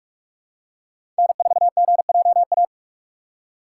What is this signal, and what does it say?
Morse code, heard as a keyed tone, about 45 wpm, 700 Hz. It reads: N4GJA
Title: USA Call Signs - 5 Characters - 2x Character Spacing - 45wpm